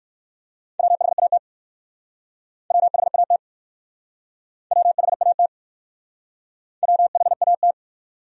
Morse code, heard as a keyed tone, about 64 wpm, 700 Hz. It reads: WHAT WHAT WHAT WHAT